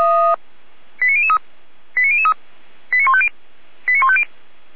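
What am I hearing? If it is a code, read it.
Five-tone selective call (ZVEI), heard as quick runs of five tones, tones of about 70 ms. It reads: 89003 89003 89150 89150
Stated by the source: ZVEI five-tone call 89501 received on an SDR.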